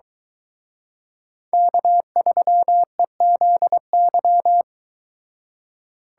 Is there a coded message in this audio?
Yes: Morse code from 1.5 to 4.6 s, heard as a keyed tone, about 22 wpm, 700 Hz. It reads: K3EZY